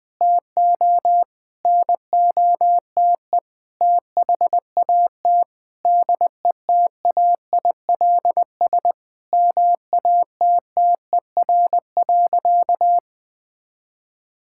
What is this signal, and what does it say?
Morse code, heard as a keyed tone, about 21 wpm, 700 Hz. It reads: TO NOTE THAT DETAILS MATTER.
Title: Sentences with 5-Word Phrases - 20wpm